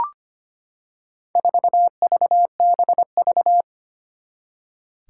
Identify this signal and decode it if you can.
Morse code, heard as a keyed tone, about 25 wpm, 700 Hz. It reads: 4VBV
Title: Four Challenging Letter-Number Combinations 25wpm